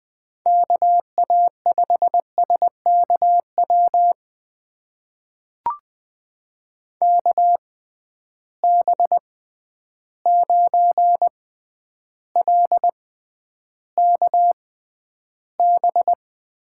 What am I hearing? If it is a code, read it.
Morse code, heard as a keyed tone, about 20 wpm, spaced as for 10 wpm, 700 Hz. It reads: KA5SKW KB9LKB
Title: USA Call Signs - 6 Characters - 6x Character Spacing - 20wpm